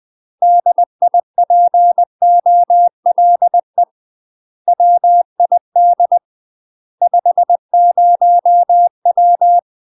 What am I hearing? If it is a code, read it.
Morse code, heard as a keyed tone, about 20 wpm, 700 Hz. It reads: DIPOLE WID 50W
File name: MASTER QSOs TRAINING/20WPM